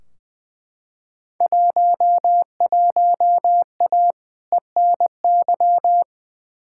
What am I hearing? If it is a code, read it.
Morse code, heard as a keyed tone, about 20 wpm, 700 Hz. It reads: 11A ENY
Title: Field Day Exchanges Encore - 20wpm